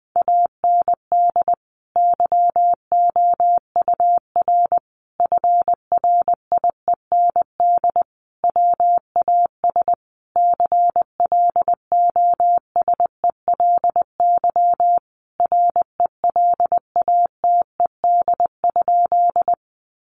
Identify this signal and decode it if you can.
Morse code, heard as a keyed tone, about 20 wpm, 700 Hz. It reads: AND YOUR FRIEND WAS CLOSELY RELATED?
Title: Sentences from Top 1400 Words 20wpm